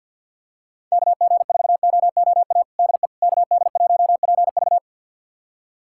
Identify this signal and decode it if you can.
Morse code, heard as a keyed tone, about 51 wpm, 700 Hz. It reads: KG4OOA DE KD1PV
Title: USA Callsign DE Callsign 50wpm